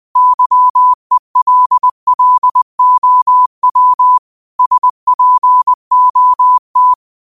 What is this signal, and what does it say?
Morse code, heard as a keyed tone, about 20 wpm, 1000 Hz. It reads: YELLOW SPOT